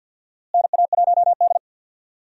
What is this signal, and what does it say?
Morse code, heard as a keyed tone, about 47 wpm, 700 Hz. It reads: NA1D
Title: USA Call Signs - 4 Characters - Repeated 3x - 50wpm